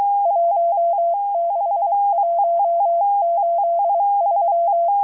Morse code, heard as a keyed tone, about 23 wpm, 680 Hz. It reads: 16183